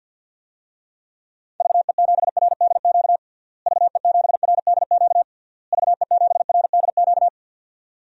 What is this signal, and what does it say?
Morse code, heard as a keyed tone, about 50 wpm, 700 Hz. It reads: VE7RDQ VE7RDQ VE7RDQ
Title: DXCC-100 Call Signs - 6 Characters - Repeated 3x - 50wpm